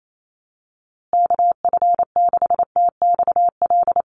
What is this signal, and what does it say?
Morse code, heard as a keyed tone, about 28 wpm, 700 Hz. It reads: KF6TXL
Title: USA Call Signs Repeated 3x 28wpm